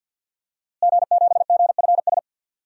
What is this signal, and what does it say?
Morse code, heard as a keyed tone, about 50 wpm, 700 Hz. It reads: G7GFS
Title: DXCC-100 Call Signs - 5 Characters - Repeated 2x - 50wpm